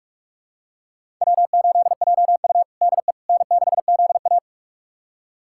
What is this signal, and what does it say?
Morse code, heard as a keyed tone, about 41 wpm, 700 Hz. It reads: W8JU DE N6ZA
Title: USA Callsign DE Callsign - 4 Characters - 45wpm